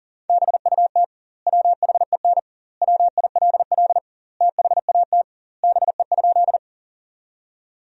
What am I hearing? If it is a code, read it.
Morse code, heard as a keyed tone, about 40 wpm, 700 Hz. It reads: BUT WHEN WILL THAT BE?
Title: 2x Word-Spacing - Sentences from Top 100 Words - 40wpm